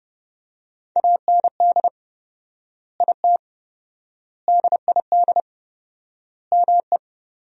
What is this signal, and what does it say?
Morse code, heard as a keyed tone, about 32 wpm, 700 Hz. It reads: AND IT DID ME
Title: 4x Word-Spacing - Sentences from Top 200 Words - 30wpm